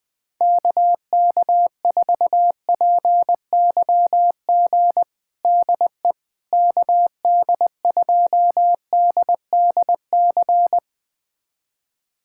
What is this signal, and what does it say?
Morse code, heard as a keyed tone, about 20 wpm, 700 Hz. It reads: KK4PYG DE KD2DDC
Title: USA Callsign DE Callsign 20wpm